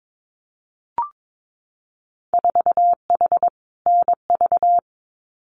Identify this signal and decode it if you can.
Morse code, heard as a keyed tone, about 22 wpm, 700 Hz. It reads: 4H NV
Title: Winter Field Day Exchanges Encore 22wpm